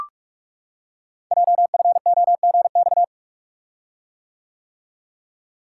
Morse code, heard as a keyed tone, about 44 wpm, 700 Hz. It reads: JFOGX
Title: Code Groups - 5 Letters - 45wpm